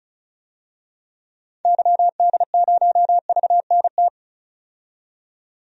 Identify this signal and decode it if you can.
Morse code, heard as a keyed tone, about 36 wpm, 700 Hz. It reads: YD0VNT
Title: DXCC-100 Call Signs - 6 Characters - 35wpm